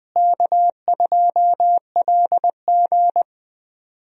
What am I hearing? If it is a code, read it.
Morse code, heard as a keyed tone, about 20 wpm, 700 Hz. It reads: K2LG
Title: USA Call Signs - 4 Characters - Repeated 2x - 20wpm